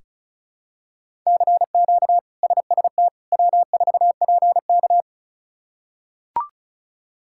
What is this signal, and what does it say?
Morse code, heard as a keyed tone, about 34 wpm, 700 Hz. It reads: CQ SST W4PK
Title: K1USN SST QSO Practice - 35wpm